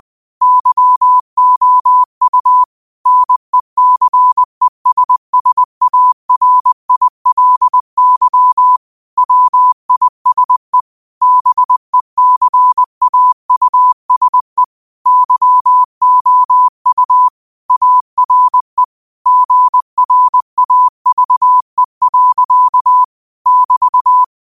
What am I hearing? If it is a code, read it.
Morse code, heard as a keyed tone, about 20 wpm, 1000 Hz. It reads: YOU NECESSARILY WISE BECAUSE YOU ARE GRAVE. <BT>
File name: qotdfiles / 20wpm